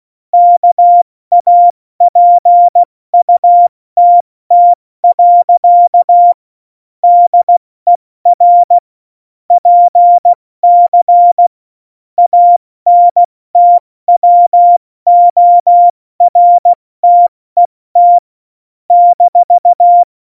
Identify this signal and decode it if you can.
Morse code, heard as a keyed tone, about 16 wpm, 700 Hz. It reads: KAPUTT. DER PC ANTWORTET -
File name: Morsekurs / Witze / Witz 22. Januar 2019, 70 F80